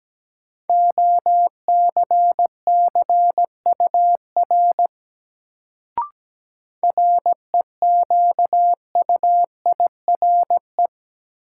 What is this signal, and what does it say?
Morse code, heard as a keyed tone, about 17 wpm, 700 Hz. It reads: OCCUR REQUIRE
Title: Sentences from Top 900 Words - New Word Drill - 17wpm